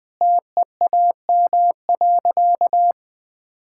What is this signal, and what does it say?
Morse code, heard as a keyed tone, about 20 wpm, 700 Hz. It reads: TEAM.
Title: Sentences with 3-Word Phrases - Spoken Beginning - 20wpm